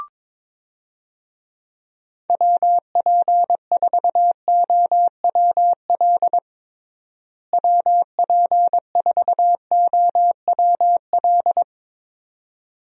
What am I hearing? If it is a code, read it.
Morse code, heard as a keyed tone, about 22 wpm, 700 Hz. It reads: WP4OWL WP4OWL
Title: USA Call Signs - 6 Characters - Repeated 2x - 22wpm